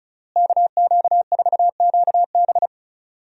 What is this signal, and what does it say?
Morse code, heard as a keyed tone, about 36 wpm, 700 Hz. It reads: KQ4QB
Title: USA Call Signs - 5 Characters - Repeated 2x - 35wpm